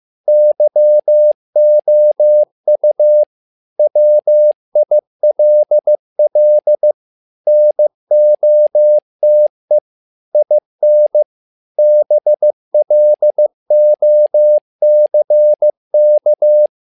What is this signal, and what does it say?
Morse code, heard as a keyed tone, about 15 wpm, 600 Hz. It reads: YOU WILL NOTE IN BLOCK
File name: learning center / tower trouble / tower trouble 15wpm